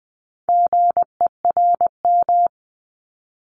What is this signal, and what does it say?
Morse code, heard as a keyed tone, about 20 wpm, 700 Hz. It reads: GERM
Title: Multi-letter Phonemes Repeated 3x 20wpm